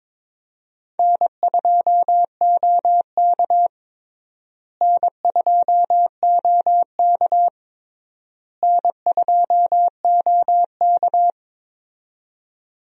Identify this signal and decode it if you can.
Morse code, heard as a keyed tone, about 22 wpm, 700 Hz. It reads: N2OK N2OK N2OK
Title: USA Call Signs - 4 Characters - Repeated 3x - 22wpm